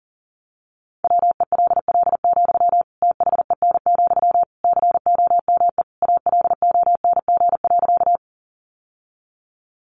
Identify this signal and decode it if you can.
Morse code, heard as a keyed tone, about 40 wpm, 700 Hz. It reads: WELL, THEN, COME ALONG.
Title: Sentences from Top 700 Words 40wpm